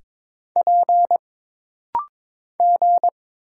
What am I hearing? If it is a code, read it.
Morse code, heard as a keyed tone, about 23 wpm, 700 Hz. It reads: P G
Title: Single Letters - Rapid-Fire 22wpm